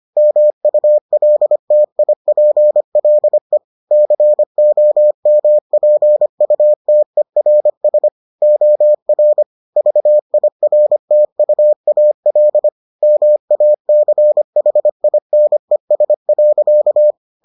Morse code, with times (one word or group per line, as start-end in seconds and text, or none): 0.2-3.6 MULTIPLE
3.9-8.1 COMPUTERS
8.4-9.4 OR
9.8-12.7 VIRTUAL
13.0-17.1 MACHINES.